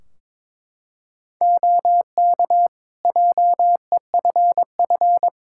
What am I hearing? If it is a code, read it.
Morse code, heard as a keyed tone, about 22 wpm, 700 Hz. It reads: OK JEFF
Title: K1USN SST QSO Practice 22wpm